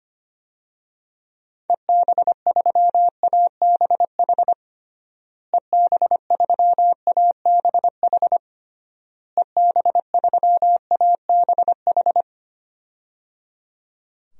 Morse code, heard as a keyed tone, about 25 wpm, 700 Hz. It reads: EB3ABH EB3ABH EB3ABH
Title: DXCC-100 Call Signs - 6 Characters - Repeated 3x - 25wpm